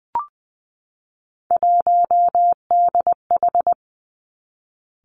Morse code, heard as a keyed tone, about 20 wpm, 700 Hz. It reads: 1DH
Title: Three Challenging Letter-Number Combinations 20wpm